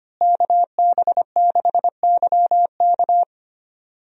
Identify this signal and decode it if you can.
Morse code, heard as a keyed tone, about 25 wpm, 700 Hz. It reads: KB6YK
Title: USA Call Signs - 5 Characters - Repeated 3x - 25wpm